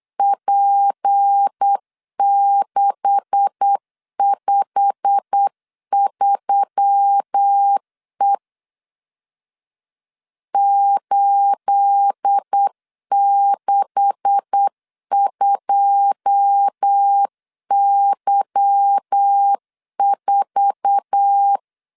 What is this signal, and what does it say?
Morse code, heard as a keyed tone, about 8 wpm, 800 Hz. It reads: P653E 862Y4
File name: LEZIONI / 9 286 8wpm 3425